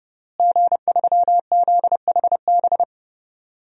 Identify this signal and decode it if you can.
Morse code, heard as a keyed tone, about 30 wpm, 700 Hz. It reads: G3ZHB
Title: DXCC-100 Call Signs - 5 Characters - Repeated 3x - 30wpm